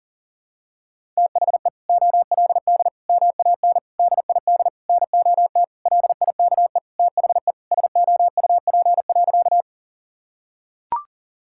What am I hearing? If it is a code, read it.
Morse code, heard as a keyed tone, about 40 wpm, 700 Hz. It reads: THE OLD MAN DID NOT LIKE THE SOUP.
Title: Sentences from Top 1000 Words - Review Part 6 - 40wpm